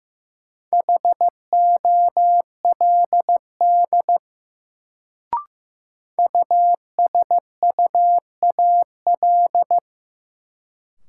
Morse code, heard as a keyed tone, about 15 wpm, 700 Hz. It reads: HOLD USUAL
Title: Sentences from Top 400 Words - New Word Drill - 15wpm